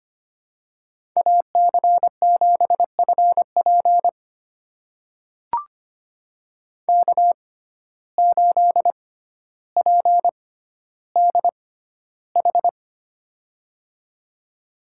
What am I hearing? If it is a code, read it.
Morse code, heard as a keyed tone, about 25 wpm, spaced as for 13 wpm, 700 Hz. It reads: AC7FP K8PDH
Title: USA Call Signs - 5 Characters - 6x Character Spacing - 25wpm